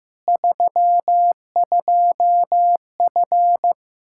Morse code, heard as a keyed tone, about 15 wpm, 700 Hz. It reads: 32F
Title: K1USN SST QSO Practice 15wpm